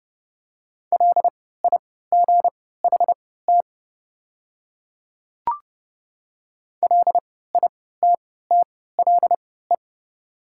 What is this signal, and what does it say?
Morse code, heard as a keyed tone, about 29 wpm, spaced as for 17 wpm, 700 Hz. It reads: LIGHT LITTLE
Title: 3x Character Spacing - Top 200 Words - 30wpm